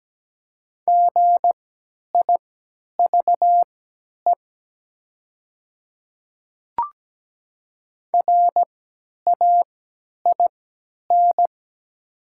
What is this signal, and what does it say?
Morse code, heard as a keyed tone, about 17 wpm, spaced as for 10 wpm, 700 Hz. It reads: GIVE RAIN
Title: 3x Character Spacing - Top 300 Words - 17wpm